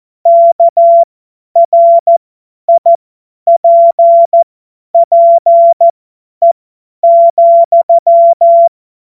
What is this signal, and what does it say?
Morse code, heard as a keyed tone, about 14 wpm, spaced as for 10 wpm, 680 Hz. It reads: KRIPPE,
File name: Morsekurs / Witze / Witz 02. November 2021, 50 F70